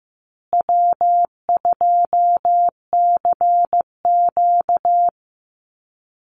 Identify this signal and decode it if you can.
Morse code, heard as a keyed tone, about 14 wpm, 700 Hz. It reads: W2CQ
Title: Easy Call Signs Repeated 2x - 15wpm